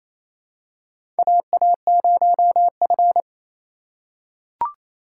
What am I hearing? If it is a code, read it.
Morse code, heard as a keyed tone, about 28 wpm, 700 Hz. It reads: AA0F
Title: USA Call Signs - 4 Characters - 2x Character Spacing - 28wpm